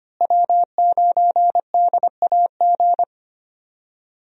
Morse code, heard as a keyed tone, about 25 wpm, 700 Hz. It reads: W9DAG
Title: USA Call Signs - 5 Characters - Repeated 3x - 25wpm